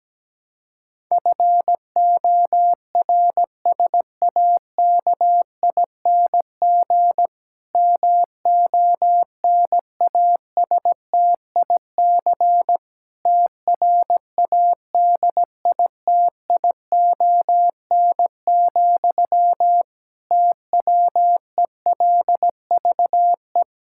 Morse code, heard as a keyed tone, about 17 wpm, 700 Hz. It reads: FORSAKING MONASTIC TRADITION, TWELVE
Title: Pangram Challenge v2 - 17wpm